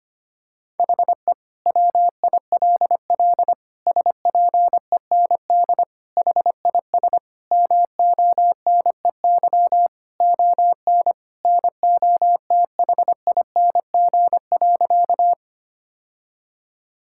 Morse code, heard as a keyed tone, about 25 wpm, 700 Hz. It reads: HE WILL SPEND HIS MONEY ON NOTHING.
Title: Sentences from Top 1000 Words - Review Part 5 - 25wpm